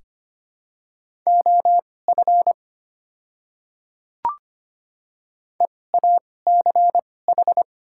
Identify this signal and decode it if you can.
Morse code, heard as a keyed tone, about 25 wpm, spaced as for 18 wpm, 690 Hz. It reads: OF EACH
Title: 2x Character Spacing - Top 100 Words - 25wpm